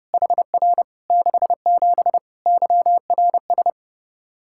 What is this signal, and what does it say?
Morse code, heard as a keyed tone, about 32 wpm, 700 Hz. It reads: HR 67 YRS
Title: K1USN SST QSO Practice 30wpm